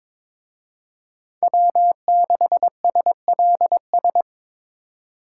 Morse code, heard as a keyed tone, about 25 wpm, 700 Hz. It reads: W6SLS
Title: Speed Racing - USA 1x3 Call Signs 22wpm